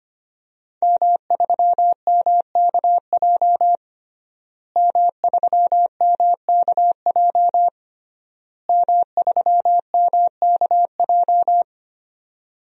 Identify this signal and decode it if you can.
Morse code, heard as a keyed tone, about 25 wpm, 700 Hz. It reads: M3MKJ M3MKJ M3MKJ
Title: DXCC-100 Call Signs - 5 Characters - Repeated 3x - 25wpm